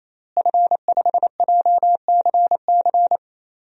Morse code, heard as a keyed tone, about 27 wpm, 700 Hz. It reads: F5JCC
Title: DXCC-100 Call Signs - 5 Characters - Repeated 3x - 28wpm